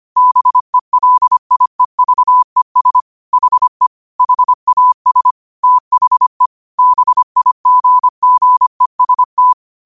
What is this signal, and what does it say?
Morse code, heard as a keyed tone, about 25 wpm, 1000 Hz. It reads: BELIEVES HE HAS THE BIGGEST
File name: qotdfiles / 25wpm